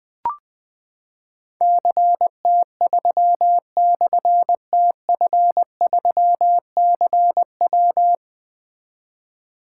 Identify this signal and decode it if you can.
Morse code, heard as a keyed tone, about 20 wpm, 700 Hz. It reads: CT3/TF3CW
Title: Difficult Call Signs 20wpm